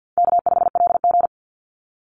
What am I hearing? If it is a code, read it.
Morse code, heard as a keyed tone, about 50 wpm, 700 Hz. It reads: K5LG
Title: Call Signs - 50wpm